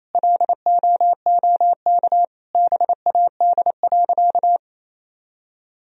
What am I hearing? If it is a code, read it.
Morse code, heard as a keyed tone, about 28 wpm, 700 Hz. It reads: LOOK BAD.
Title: Sentences from Top 1000 Words - Review Part 3 - 28wpm